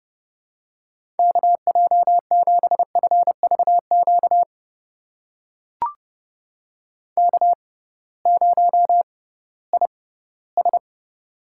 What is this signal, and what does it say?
Morse code, heard as a keyed tone, about 30 wpm, spaced as for 18 wpm, 700 Hz. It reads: KJ7FVQ K0IS
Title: USA Call Signs - 6x Character Spacing - 30wpm